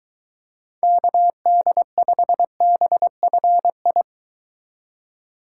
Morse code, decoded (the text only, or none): KD5BFI